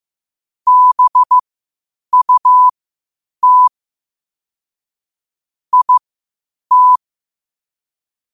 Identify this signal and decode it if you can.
Morse code, heard as a keyed tone, about 15 wpm, spaced as for 8 wpm, 1000 Hz. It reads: BUT IT
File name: qotdfiles / 8wpm